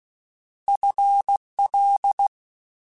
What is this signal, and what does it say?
Morse code, heard as a keyed tone, about 16 wpm, 800 Hz. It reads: FL
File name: FL / FL mp3s / FL Prefixes 3x2 Twice @15